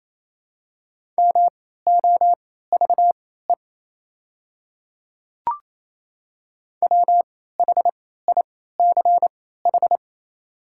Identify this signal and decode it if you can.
Morse code, heard as a keyed tone, about 28 wpm, spaced as for 16 wpm, 700 Hz. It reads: MOVE WHICH
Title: 3x Character Spacing - Top 300 Words - 28wpm